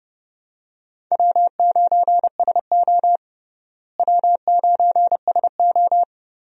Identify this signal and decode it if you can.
Morse code, heard as a keyed tone, about 29 wpm, 700 Hz. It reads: W9SO W9SO
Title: USA Call Signs - 4 Characters - Repeated 3x - 30wpm